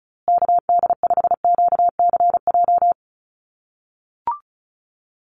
Morse code, heard as a keyed tone, about 34 wpm, 700 Hz. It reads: KD5QCJ
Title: USA Call Signs Repeated 3x 35wpm